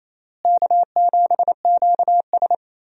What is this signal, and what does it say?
Morse code, heard as a keyed tone, about 28 wpm, 700 Hz. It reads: K7QS
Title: USA Call Signs - 4 Characters - Repeated 3x - 28wpm